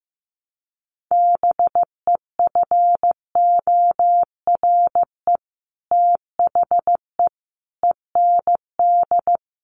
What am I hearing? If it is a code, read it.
Morse code, heard as a keyed tone, about 15 wpm, 700 Hz. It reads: BEFORE THE END
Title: Sets of 5 Words Encore 15wpm